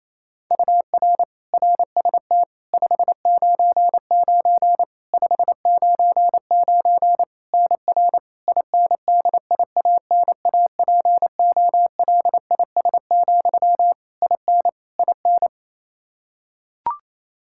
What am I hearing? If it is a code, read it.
Morse code, heard as a keyed tone, about 28 wpm, 700 Hz. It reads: UR RST 599 599 NR INDIANAPOLIS, IN IN
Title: Practice QSO - Exchange 1 - Beginning - 28wpm